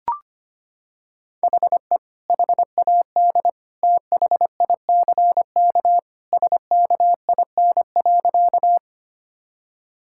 Morse code, heard as a keyed tone, about 25 wpm, 700 Hz. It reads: HE HAD THICK SKIN.
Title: Sentences from Top 1600 Words 25wpm